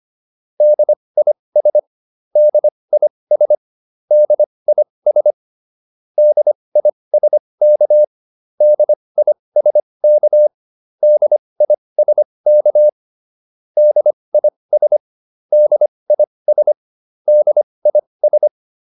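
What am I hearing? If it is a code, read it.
Morse code, heard as a keyed tone, about 25 wpm, spaced as for 20 wpm, 600 Hz. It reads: DIS DIS DIS DISK DISK DISK DIS DIS DIS
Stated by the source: Advanced Morse practice drill.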